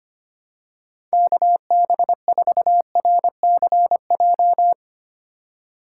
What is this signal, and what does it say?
Morse code, heard as a keyed tone, about 25 wpm, 700 Hz. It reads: KB4RCJ
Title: USA Call Signs - 6 Characters - 3x Character Spacing - 25wpm